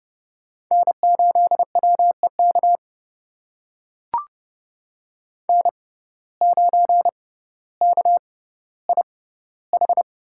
Morse code, heard as a keyed tone, about 32 wpm, spaced as for 16 wpm, 700 Hz. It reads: N8WEK N9KIH
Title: USA Call Signs - 5 Characters - 6x Character Spacing - 30wpm